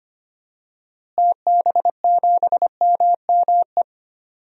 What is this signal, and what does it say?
Morse code, heard as a keyed tone, about 25 wpm, 690 Hz. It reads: TB7MME
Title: DXCC-100 Call Signs - 6 Characters - Repeated 3x - 25wpm